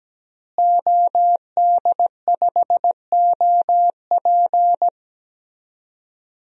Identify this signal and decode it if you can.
Morse code, heard as a keyed tone, about 17 wpm, 700 Hz. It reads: OD5OP
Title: DXCC-100 Call Signs - 5 Characters - Repeated 2x - 17wpm